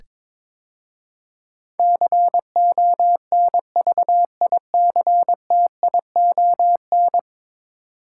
Morse code, heard as a keyed tone, about 22 wpm, 700 Hz. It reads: CONVICTION
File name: practice / 22wpm / 10 Letter Words - First 3 Letters Given - 22wpm